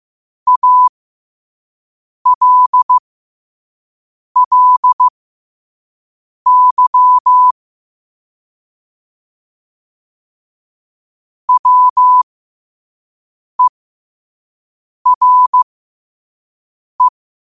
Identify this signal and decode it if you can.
Morse code, heard as a keyed tone, about 15 wpm, spaced as for 5 wpm, 1000 Hz. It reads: ALLY WERE
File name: qotdfiles / 5wpm